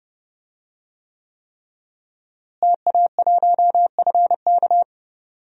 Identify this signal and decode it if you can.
Morse code, heard as a keyed tone, about 30 wpm, 700 Hz. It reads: TA1FK
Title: DXCC-100 Call Signs - 5 Characters - Repeated 3x - 30wpm